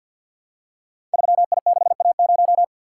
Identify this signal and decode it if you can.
Morse code, heard as a keyed tone, about 47 wpm, 700 Hz. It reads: 3I6A0
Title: Code Groups - 5 Letter:Numbers - 50wpm